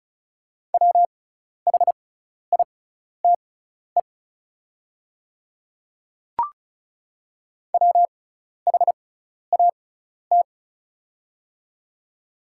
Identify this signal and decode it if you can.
Morse code, heard as a keyed tone, about 35 wpm, spaced as for 12 wpm, 690 Hz. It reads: WHITE WHAT
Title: 6x Character Spacing - Top 300 Words - 35wpm